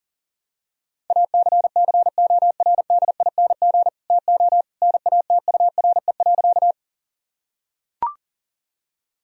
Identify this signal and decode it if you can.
Morse code, heard as a keyed tone, about 37 wpm, 700 Hz. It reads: ACCORDING TO NATURE.
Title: Sentences from Top 1000 Words - Review Part 4 - 40wpm